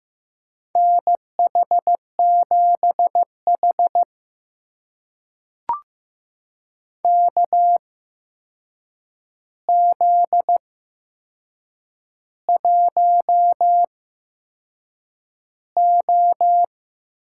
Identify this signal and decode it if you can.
Morse code, heard as a keyed tone, about 15 wpm, spaced as for 6 wpm, 700 Hz. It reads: NH7H KZ1O
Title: USA Call Signs - 4 Characters - 8x Character Spacing - 15wpm